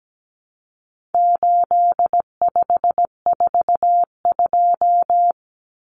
Morse code, heard as a keyed tone, about 17 wpm, 700 Hz. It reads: 8542
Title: Common Cut Numbers - 4-Digit 17wpm